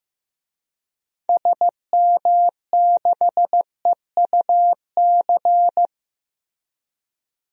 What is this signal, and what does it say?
Morse code, heard as a keyed tone, about 15 wpm, 700 Hz. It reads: SM6EUC
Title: DXCC-100 Call Signs - 6 Characters - Repeated 2x - 15wpm